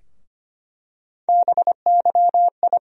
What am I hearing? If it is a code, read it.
Morse code, heard as a keyed tone, about 25 wpm, 700 Hz. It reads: BYI
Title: USA Call Sign Postfixes - 3 Letters 25wpm